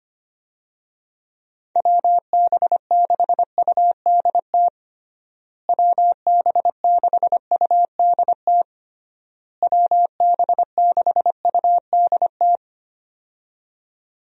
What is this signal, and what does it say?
Morse code, heard as a keyed tone, about 25 wpm, 700 Hz. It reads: WB6UDT WB6UDT WB6UDT
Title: USA Call Signs - 6 Characters - Repeated 3x - 25wpm